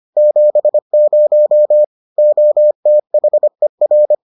Morse code, heard as a keyed tone, about 25 wpm, 600 Hz. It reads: 70 OTHER